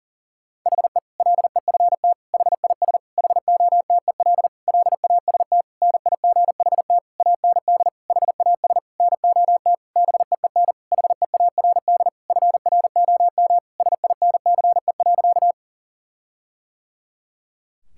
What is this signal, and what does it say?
Morse code, heard as a keyed tone, about 40 wpm, 700 Hz. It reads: HE LEFT HIS HOTEL LAST NIGHT AND HAS NOT BEEN HEARD FROM SINCE.
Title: Sentences from Top 1600 Words 40wpm